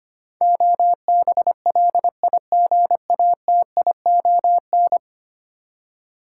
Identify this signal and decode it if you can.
Morse code, heard as a keyed tone, about 25 wpm, 700 Hz. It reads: OBLIGATION
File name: Ten Letter Words - 25wpm